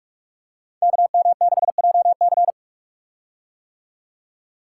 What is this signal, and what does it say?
Morse code, heard as a keyed tone, about 44 wpm, 700 Hz. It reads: KM6JC
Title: Code Groups - 5 Letter:Numbers - 45wpm